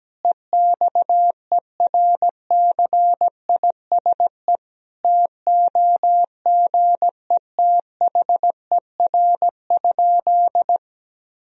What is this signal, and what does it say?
Morse code, heard as a keyed tone, about 17 wpm, 700 Hz. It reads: EXERCISE TOGETHER?
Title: Sentences from Top 1000 Words - Review Part 4 - 17wpm